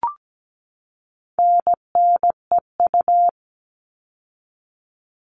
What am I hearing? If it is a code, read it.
Morse code, heard as a keyed tone, about 17 wpm, 700 Hz. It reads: NNEU
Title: All Cut Numbers - 4-Digit 17wpm